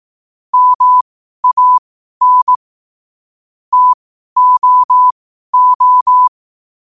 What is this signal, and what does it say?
Morse code, heard as a keyed tone, about 18 wpm, spaced as for 12 wpm, 1000 Hz. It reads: MAN TOO
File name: qotdfiles / 12wpm